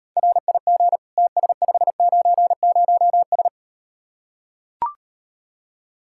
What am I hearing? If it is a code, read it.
Morse code, heard as a keyed tone, about 39 wpm, 700 Hz. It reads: RIG TS590S